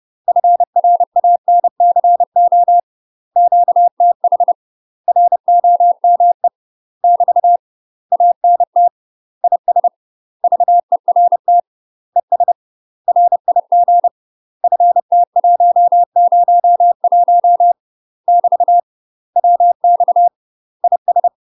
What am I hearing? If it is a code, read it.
Morse code, heard as a keyed tone, about 30 wpm, 700 Hz. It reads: FRANCO QTH ROME <BT> ANT IS VERT ES RIG FT101 <BT> WX IS
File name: UK MASTER QSOs TRAINING / 30WPM / UK MASTER QSOs 0044